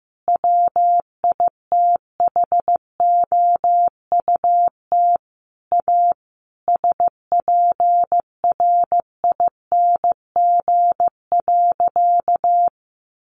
Morse code, heard as a keyed tone, about 15 wpm, 700 Hz. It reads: WITHOUT A SPRING.
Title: Sentences from Top 1200 Words 15wpm